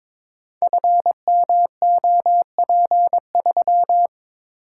Morse code, heard as a keyed tone, about 22 wpm, 700 Hz. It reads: FMOP3
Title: Code Groups - 5 Letter:Numbers - 22wpm